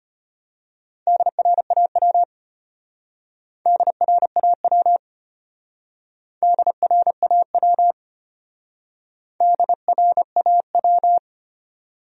Morse code, heard as a keyed tone, about 32 wpm, 700 Hz. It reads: DRAW DRAW DRAW DRAW